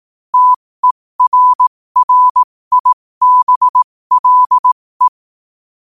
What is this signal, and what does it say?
Morse code, heard as a keyed tone, about 18 wpm, spaced as for 15 wpm, 1000 Hz. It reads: TERRIBLE
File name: qotdfiles / 15wpm